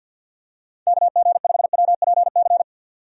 Morse code, heard as a keyed tone, about 51 wpm, 700 Hz. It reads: KG5WPC
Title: USA Call Signs - 6 Characters - 50wpm